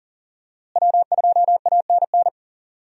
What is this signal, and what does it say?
Morse code, heard as a keyed tone, about 40 wpm, 700 Hz. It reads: W2ANN
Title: USA Call Signs - 5 Characters - Repeated 3x - 40wpm